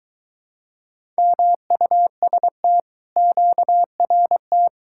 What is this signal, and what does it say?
Morse code, heard as a keyed tone, about 24 wpm, 700 Hz. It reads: MUST QRT